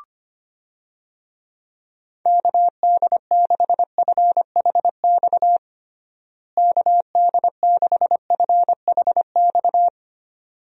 Morse code, heard as a keyed tone, about 25 wpm, 700 Hz. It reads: KD6FHX KD6FHX
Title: USA Call Signs - 6 Characters - Repeated 2x - 25wpm